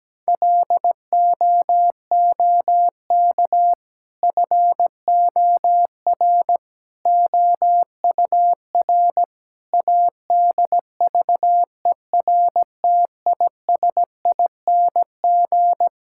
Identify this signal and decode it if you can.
Morse code, heard as a keyed tone, about 17 wpm, 700 Hz. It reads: LOOK FOR OUR ADVERTISING
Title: Sentences from Top 1000 Words - Review Part 6 - 17wpm